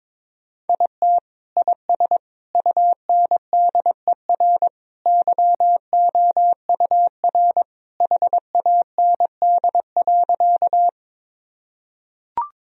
Morse code, heard as a keyed tone, about 22 wpm, 700 Hz. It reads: IT IS UNDER YOUR HAND.
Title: Sentences from Top 1000 Words - Review Part 1 - 22wpm